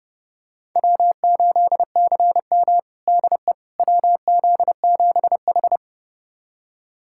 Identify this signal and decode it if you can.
Morse code, heard as a keyed tone, about 32 wpm, 700 Hz. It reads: W8CM DE WZ7H
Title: USA Callsign DE Callsign - 4 Characters - 30wpm